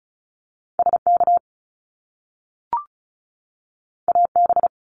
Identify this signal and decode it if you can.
Morse code, heard as a keyed tone, about 36 wpm, 700 Hz. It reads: SK AB
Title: Canadian Provinces and Territories 35wpm